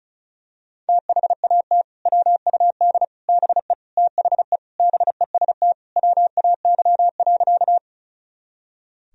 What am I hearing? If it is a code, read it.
Morse code, heard as a keyed tone, about 36 wpm, 700 Hz. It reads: THAT WUD BE THE BEST WAY.
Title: Sentences from Top 400 Words and Abbreviations 35wpm